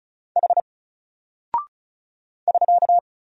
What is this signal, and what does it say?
Morse code, heard as a keyed tone, about 36 wpm, 700 Hz. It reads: H <SK>
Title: Single Character (Letter:Number:Punctuation:Prosign) Alt 35wpm